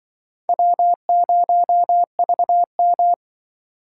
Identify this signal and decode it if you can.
Morse code, heard as a keyed tone, about 24 wpm, 700 Hz. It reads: W0VM
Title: Speed Racing - USA 1x2 Call Signs 20wpm